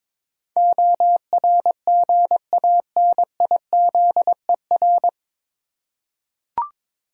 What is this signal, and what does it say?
Morse code, heard as a keyed tone, about 22 wpm, 700 Hz. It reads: ORGANIZER
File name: 9 Letter Words - First 3 Letters Given - 22wpm